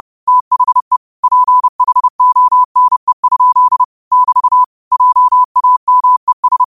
Morse code, heard as a keyed tone, about 30 wpm, 1000 Hz. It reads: THE PHONE? <BT> JAMES